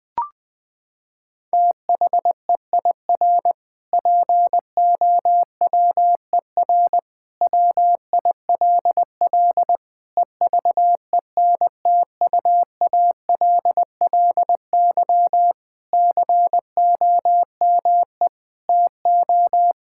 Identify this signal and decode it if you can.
Morse code, heard as a keyed tone, about 20 wpm, 700 Hz. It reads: THEIR POWER WILL EVENTUALLY COME TO
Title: Sentences from Top 1000 Words - Review Part 6 - 20wpm